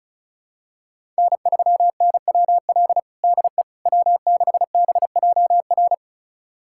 Morse code, heard as a keyed tone, about 36 wpm, 700 Hz. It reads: N3NWL DE W6BJR
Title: USA Callsign DE Callsign - 5 Characters - 35wpm